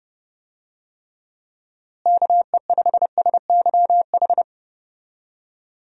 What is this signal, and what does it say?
Morse code, heard as a keyed tone, about 32 wpm, 700 Hz. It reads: KE5SYH
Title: USA Call Signs - 6 Characters - Repeated 2x - 30wpm